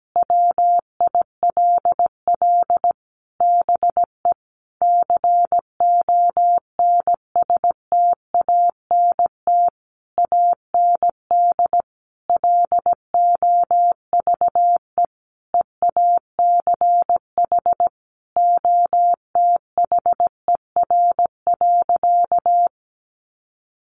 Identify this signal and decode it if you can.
Morse code, heard as a keyed tone, about 17 wpm, 700 Hz. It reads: WILL BE CONSTANT AND LOVE EACH OTHER.
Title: Sentences from Top 1500 Words 17wpm